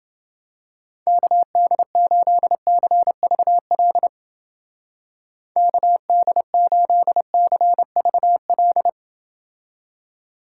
Morse code, heard as a keyed tone, about 28 wpm, 700 Hz. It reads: KD8CVL KD8CVL